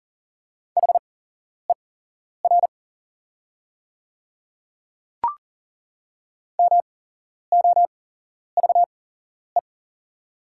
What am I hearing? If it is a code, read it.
Morse code, heard as a keyed tone, about 40 wpm, spaced as for 11 wpm, 690 Hz. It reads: HER MOVE
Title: 8x Character Spacing - Top 200 Words - 40wpm